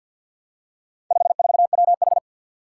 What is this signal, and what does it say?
Morse code, heard as a keyed tone, about 46 wpm, 700 Hz. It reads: 54WH
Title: Four Challenging Letter-Number Combinations 50wpm